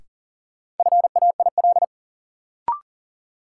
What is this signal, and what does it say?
Morse code, heard as a keyed tone, about 40 wpm, 700 Hz. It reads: FAIL
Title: Sentences from Top 1100 Words - New Word Drill - 40wpm